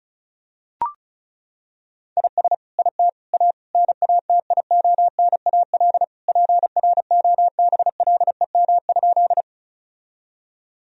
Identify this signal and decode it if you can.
Morse code, heard as a keyed tone, about 36 wpm, 700 Hz. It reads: IS IT A NATIONAL PROBLEM?
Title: Sentences from Top 1000 Words - Review Part 4 - 35wpm